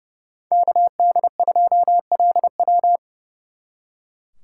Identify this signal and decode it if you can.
Morse code, heard as a keyed tone, about 35 wpm, 700 Hz. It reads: KD2LW
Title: Speed Racing - USA Call Signs 30wpm